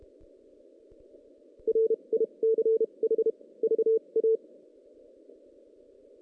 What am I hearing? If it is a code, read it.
Morse code, heard as a keyed tone, about 30 wpm, spaced as for 26 wpm, 440 Hz. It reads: RICH VA
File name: 2016 JAN CW NAQP / 2016 Jan naqp k5ot - 0-6